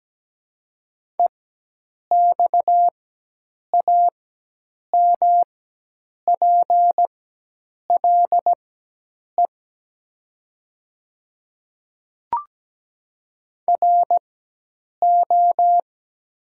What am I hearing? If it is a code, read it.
Morse code, heard as a keyed tone, about 17 wpm, spaced as for 8 wpm, 700 Hz. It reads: EXAMPLE RO
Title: 4x Character Spacing - Top 500 Words - 17wpm